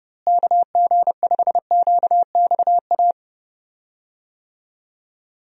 Morse code, heard as a keyed tone, about 32 wpm, 700 Hz. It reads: KG5QXA
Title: USA Call Signs - 6 Characters - Repeated 3x - 30wpm